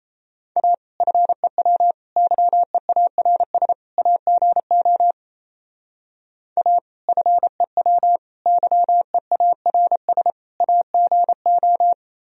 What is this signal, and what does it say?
Morse code, heard as a keyed tone, about 36 wpm, 700 Hz. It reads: A FEW YEARS AGO A FEW YEARS AGO